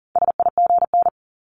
Morse code, heard as a keyed tone, about 37 wpm, 700 Hz. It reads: SIGN
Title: Multi-letter Phonemes Repeated 3x 40wpm